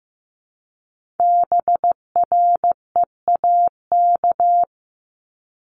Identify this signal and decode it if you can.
Morse code, heard as a keyed tone, about 15 wpm, 700 Hz. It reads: BREAK
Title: Multi-letter Phonemes Repeated 3x 15wpm